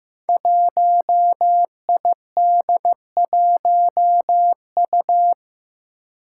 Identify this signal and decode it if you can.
Morse code, heard as a keyed tone, about 15 wpm, 700 Hz. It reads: 1ID1U
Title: Five Challenging Letter-Number Combinations 15wpm